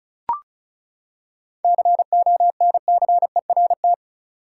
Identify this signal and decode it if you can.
Morse code, heard as a keyed tone, about 36 wpm, 700 Hz. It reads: CONCERT
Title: Seven Letter Words - 35wpm